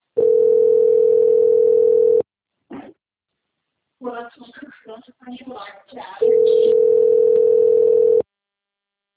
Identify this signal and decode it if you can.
Ringback tone (North America), 440 + 480 Hz, with speech.
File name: out- - - - - .612